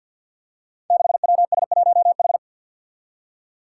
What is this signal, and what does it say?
Morse code, heard as a keyed tone, about 51 wpm, 700 Hz. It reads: 6WS1H
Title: Five Challenging Letter-Number Combinations 50wpm